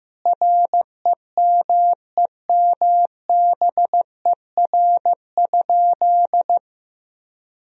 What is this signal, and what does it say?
Morse code, heard as a keyed tone, about 15 wpm, 700 Hz. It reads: REMEMBER?
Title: Sentences from Top 400 Words and Abbreviations 15wpm